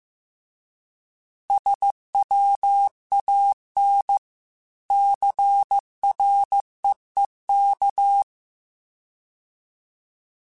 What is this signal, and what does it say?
Morse code, heard as a keyed tone, about 15 wpm, 800 Hz. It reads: SWAN CREEK